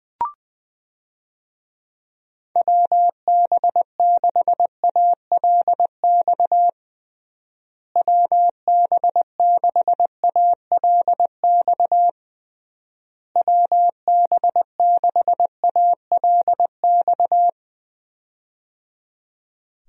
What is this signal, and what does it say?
Morse code, heard as a keyed tone, about 20 wpm, 700 Hz. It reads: WB6ALX WB6ALX WB6ALX
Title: USA Call Signs - 6 Characters - Repeated 3x - 20wpm